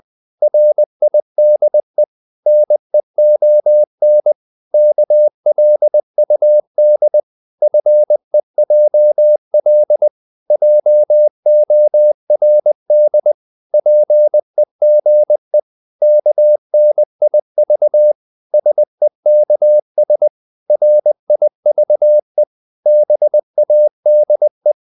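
Morse code, heard as a keyed tone, about 20 wpm, 600 Hz. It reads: RIDE NEON KLUD FEJL JORD PEGE KNIV SEKS RIVE BADE
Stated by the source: Morse code practice of short Danish words at 20 wpm, from CW med Gnister.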